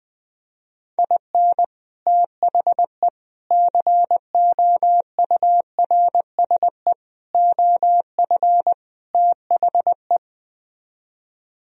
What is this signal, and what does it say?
Morse code, heard as a keyed tone, about 20 wpm, 700 Hz. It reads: IN THE COURSE OF THE